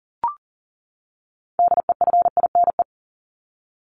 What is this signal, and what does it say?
Morse code, heard as a keyed tone, about 40 wpm, 700 Hz. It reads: DEFINE
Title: ED Suffix - Top 100 Words with Base Words 40wpm